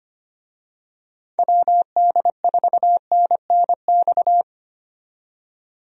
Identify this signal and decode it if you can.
Morse code, heard as a keyed tone, about 25 wpm, 700 Hz. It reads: WD4NNX
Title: USA Call Signs - 6 Characters - 25wpm